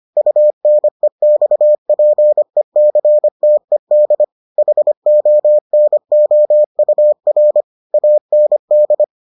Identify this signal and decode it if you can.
Morse code, heard as a keyed tone, about 25 wpm, 600 Hz. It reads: UNEXPECTED HONOUR AND